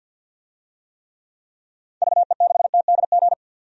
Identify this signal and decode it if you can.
Morse code, heard as a keyed tone, about 46 wpm, 700 Hz. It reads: VE6TDG
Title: DXCC-100 Call Signs - 6 Characters - 50wpm